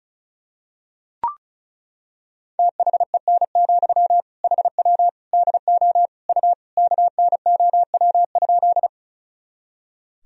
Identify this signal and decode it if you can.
Morse code, heard as a keyed tone, about 36 wpm, 700 Hz. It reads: THEN, HW DO U KNOW?
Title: Sentences from Top 100 Words and Abbreviations 35wpm